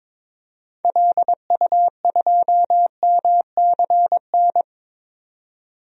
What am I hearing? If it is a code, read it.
Morse code, heard as a keyed tone, about 22 wpm, 700 Hz. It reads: LU2MCN
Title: DXCC-100 Call Signs - 6 Characters - 22wpm